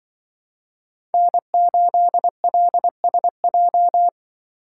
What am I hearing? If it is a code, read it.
Morse code, heard as a keyed tone, about 24 wpm, 700 Hz. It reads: N8LSJ
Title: Speed Racing - USA Call Signs 20wpm